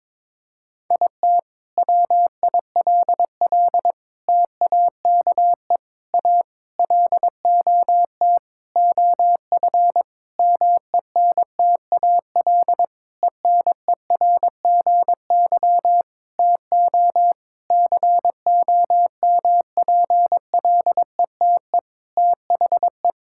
Morse code, heard as a keyed tone, about 22 wpm, 700 Hz. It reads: IT WILL TAKE A LOT OF MENTAL ENERGY TO COMPLETE THE
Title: Sentences from Top 1000 Words - Review Part 6 - 22wpm